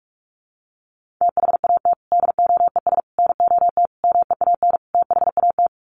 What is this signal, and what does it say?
Morse code, heard as a keyed tone, about 44 wpm, 700 Hz. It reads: THAT DOES NOT MEAN THAT
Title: Sets of 5 Words Encore 45wpm